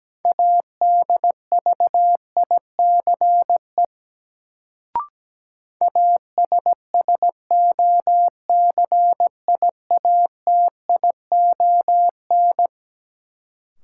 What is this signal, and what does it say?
Morse code, heard as a keyed tone, about 17 wpm, 700 Hz. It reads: ADVICE ASSOCIATION
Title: Sentences from Top 1100 Words - New Word Drill - 17wpm